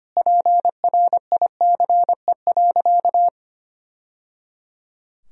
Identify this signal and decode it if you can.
Morse code, heard as a keyed tone, about 25 wpm, 700 Hz. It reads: PRICE.
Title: Sentences from Top 1000 Words - Review Part 4 - 25wpm